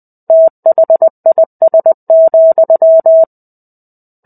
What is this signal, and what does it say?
Morse code, heard as a keyed tone, about 20 wpm, 650 Hz. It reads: THIS,
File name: 05-Chapter-05 20 WPM